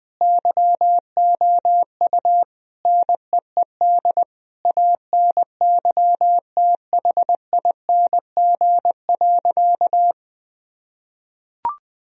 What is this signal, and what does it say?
Morse code, heard as a keyed tone, about 20 wpm, 700 Hz. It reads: YOU NEED ANYTHING.
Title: Sentences from Top 1000 Words - Review Part 1 - 20wpm